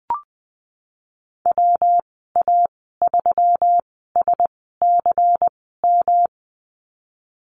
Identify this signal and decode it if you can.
Morse code, heard as a keyed tone, about 20 wpm, spaced as for 15 wpm, 700 Hz. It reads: WA3SCM
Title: 128 Lesson 21g - Number 6 - Call Signs 20wpm